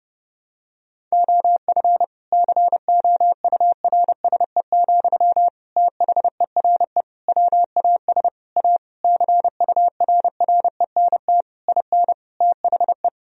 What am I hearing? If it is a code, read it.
Morse code, heard as a keyed tone, about 31 wpm, 700 Hz. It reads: OF COURSE, THERE WAS A CURRENT IN THE
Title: Sentences from Top 1000 Words - Review Part 4 - 30wpm